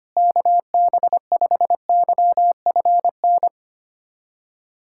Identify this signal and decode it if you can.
Morse code, heard as a keyed tone, about 25 wpm, 700 Hz. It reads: KB5YFN
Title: SOTA QSO Practice 25wpm